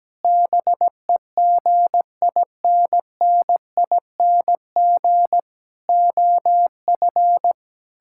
Morse code, heard as a keyed tone, about 17 wpm, 700 Hz. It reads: BEGINNING OF
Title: Sentences from Top 1000 Words - Review Part 6 - 17wpm